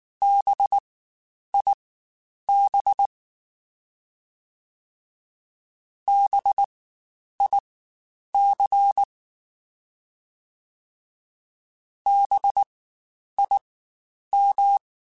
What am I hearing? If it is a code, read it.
Morse code, heard as a keyed tone, about 19 wpm, spaced as for 9 wpm, 800 Hz. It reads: BIB BIC BIM
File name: Spaced Alphas / Spaced Alphas mp3s / Bs sp @18